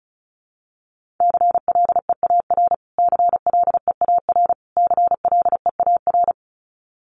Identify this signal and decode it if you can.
Morse code, heard as a keyed tone, about 34 wpm, 700 Hz. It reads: CLEAR CLEAR CLEAR
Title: Top 500 Words Repeated 35wpm